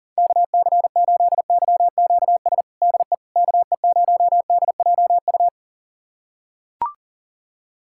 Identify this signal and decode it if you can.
Morse code, heard as a keyed tone, about 40 wpm, 700 Hz. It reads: KC8YQS DE KE0DJU
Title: USA Callsign DE Callsign 40wpm